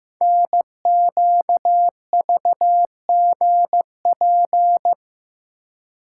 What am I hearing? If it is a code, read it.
Morse code, heard as a keyed tone, about 15 wpm, 700 Hz. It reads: NQVGP
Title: Code Groups - 5 Letters - 15wpm